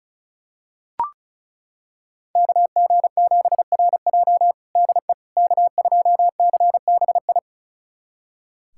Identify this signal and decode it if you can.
Morse code, heard as a keyed tone, about 36 wpm, 700 Hz. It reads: KG7RJ DE K2CBI
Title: USA Callsign DE Callsign - 5 Characters - 35wpm